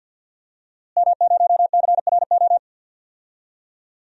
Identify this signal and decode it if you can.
Morse code, heard as a keyed tone, about 52 wpm, 700 Hz. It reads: M0CRO
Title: DXCC-100 Call Signs - 5 Characters - Repeated 2x - 50wpm